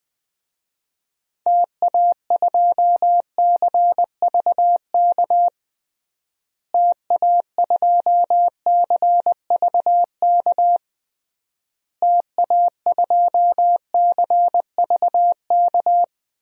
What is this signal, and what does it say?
Morse code, heard as a keyed tone, about 20 wpm, 700 Hz. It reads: TA2CVK TA2CVK TA2CVK
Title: DXCC-100 Call Signs - 6 Characters - Repeated 3x - 20wpm